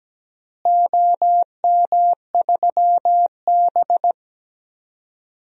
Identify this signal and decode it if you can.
Morse code, heard as a keyed tone, about 17 wpm, 700 Hz. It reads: OM3B
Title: DXCC-100 Call Signs - 4 Characters - Repeated 3x - 17wpm